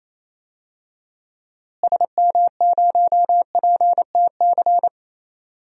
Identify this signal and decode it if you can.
Morse code, heard as a keyed tone, about 28 wpm, 700 Hz. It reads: SM0PTC
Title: DXCC-100 Call Signs - 6 Characters - Repeated 3x - 28wpm